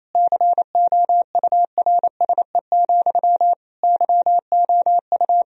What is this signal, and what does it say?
Morse code, heard as a keyed tone, about 28 wpm, 700 Hz. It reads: COURSE, YOU
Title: Sentences from Top 1000 Words - Review Part 1 - 28wpm